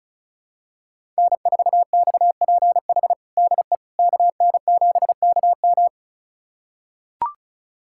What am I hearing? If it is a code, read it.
Morse code, heard as a keyed tone, about 36 wpm, 700 Hz. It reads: N4XPH DE KN7KM
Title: USA Callsign DE Callsign - 5 Characters - 35wpm